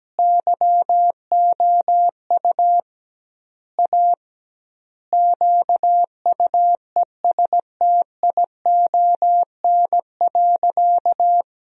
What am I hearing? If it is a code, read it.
Morse code, heard as a keyed tone, about 17 wpm, 700 Hz. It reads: YOU A QUESTION.
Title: 2x Word-Spacing - Sentences from Top 400 Words - 17wpm